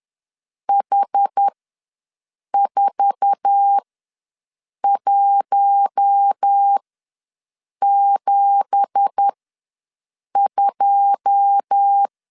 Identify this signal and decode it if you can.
Morse code, heard as a keyed tone, about 10 wpm, spaced as for 6 wpm, 800 Hz. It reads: H4172
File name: LEZIONI / 9 2867 ws 10wpm 4215